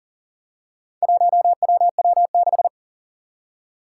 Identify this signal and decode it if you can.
Morse code, heard as a keyed tone, about 40 wpm, 700 Hz. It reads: 1WW6
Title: Four Challenging Letter-Number Combinations 40wpm